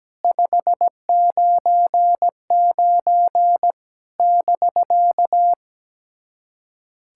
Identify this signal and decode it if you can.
Morse code, heard as a keyed tone, about 17 wpm, 700 Hz. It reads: 599 <BK>
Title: SOTA QSO Practice 17wpm